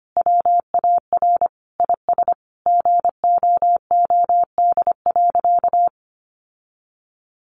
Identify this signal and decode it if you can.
Morse code, heard as a keyed tone, about 25 wpm, 700 Hz. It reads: WAR IS GOOD.
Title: Sentences from Top 1300 Words 25wpm